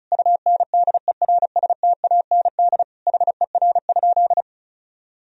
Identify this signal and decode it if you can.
Morse code, heard as a keyed tone, about 34 wpm, 700 Hz. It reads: UNDERSTAND HER?
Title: Sentences from Top 1000 Words - Review Part 2 - 35wpm